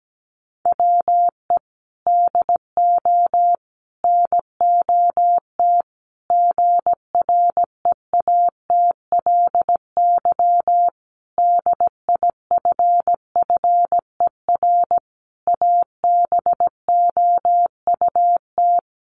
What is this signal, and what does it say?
Morse code, heard as a keyed tone, about 17 wpm, 700 Hz. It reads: WE DO NOT GREATLY DIFFER ABOUT
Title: Sentences from Top 1400 Words 17wpm